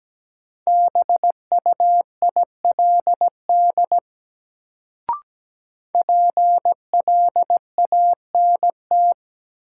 Morse code, heard as a keyed tone, about 17 wpm, 700 Hz. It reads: BUILD PLANT
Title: Sentences from Top 1000 Words - Review Part 1 - New Word Drill - 17wpm